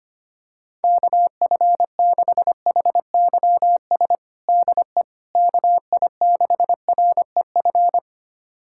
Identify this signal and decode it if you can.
Morse code, heard as a keyed tone, about 25 wpm, 700 Hz. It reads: KF6HYS DE KI6REF
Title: USA Callsign DE Callsign - 6 Characters - 25wpm